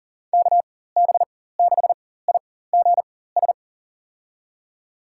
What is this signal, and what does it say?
Morse code, heard as a keyed tone, about 40 wpm, spaced as for 19 wpm, 700 Hz. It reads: KB6IGS
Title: USA Call Signs - 6 Characters - 4x Character Spacing - 40wpm